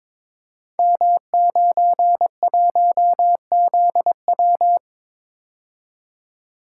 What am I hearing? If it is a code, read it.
Morse code, heard as a keyed tone, about 22 wpm, 690 Hz. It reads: M91ZW
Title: Code Groups - 5 Letter:Numbers - 22wpm